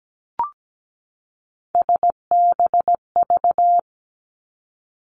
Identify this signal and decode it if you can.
Morse code, heard as a keyed tone, about 17 wpm, 700 Hz. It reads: SBV
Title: Three Challenging Letter-Number Combinations 17wpm